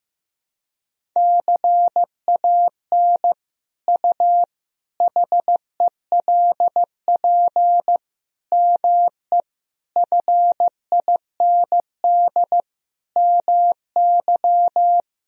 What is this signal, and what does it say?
Morse code, heard as a keyed tone, about 15 wpm, 700 Hz. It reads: CAN U HELP ME FIND MY
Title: Sentences from Top 200 Words and Abbreviations 15wpm